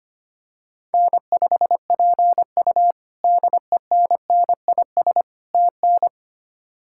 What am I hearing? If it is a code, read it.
Morse code, heard as a keyed tone, about 25 wpm, 700 Hz. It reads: N5PU DENNIS TN
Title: K1USN SST QSO Practice - 25wpm